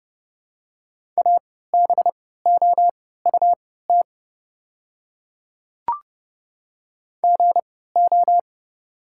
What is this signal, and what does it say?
Morse code, heard as a keyed tone, about 30 wpm, spaced as for 17 wpm, 700 Hz. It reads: ABOUT GO
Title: 3x Character Spacing - Top 100 Words - 30wpm